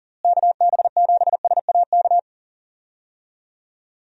Morse code, heard as a keyed tone, about 39 wpm, 700 Hz. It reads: KB7SAK
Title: USA Call Signs - 6 Characters - Repeated 2x - 40wpm